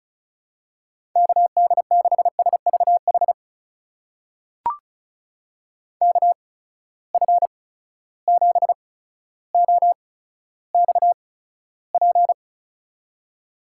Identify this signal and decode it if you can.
Morse code, heard as a keyed tone, about 34 wpm, spaced as for 15 wpm, 700 Hz. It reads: KD6SVH KF7OXP
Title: USA Call Signs - 6 Characters - 8x Character Spacing - 35wpm